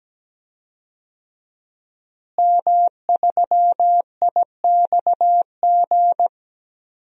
Morse code, heard as a keyed tone, about 17 wpm, 700 Hz. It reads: M3IXG
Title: DXCC-100 Call Signs - 5 Characters - Repeated 3x - 17wpm